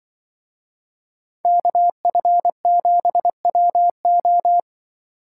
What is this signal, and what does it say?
Morse code, heard as a keyed tone, about 24 wpm, 700 Hz. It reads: KF7WO
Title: Speed Racing - USA 2x2 Call Signs 20wpm